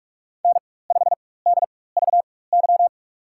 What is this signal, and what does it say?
Morse code, heard as a keyed tone, about 43 wpm, spaced as for 21 wpm, 700 Hz. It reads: N5DVY
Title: USA Call Signs - 5 Characters - 4x Character Spacing - 45wpm